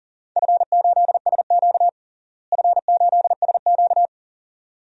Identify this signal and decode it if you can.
Morse code, heard as a keyed tone, about 40 wpm, 700 Hz. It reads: F8SQ F8SQ
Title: DXCC-100 Call Signs - 4 Characters - Repeated 2x - 40wpm